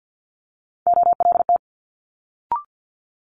Morse code, heard as a keyed tone, about 50 wpm, 700 Hz. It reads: OLT
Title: Multi-letter Phonemes Quick Fire 50wpm